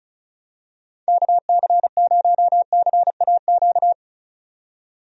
Morse code, heard as a keyed tone, about 34 wpm, 700 Hz. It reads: KC0CAQ
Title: USA Call Signs - 6 Characters - 35wpm